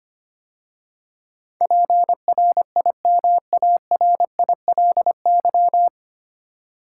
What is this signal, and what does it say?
Morse code, heard as a keyed tone, about 25 wpm, 700 Hz. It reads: PRIMARILY
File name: practice / 25wpm / 9 Letter Words - First 3 Letters Given - 25wpm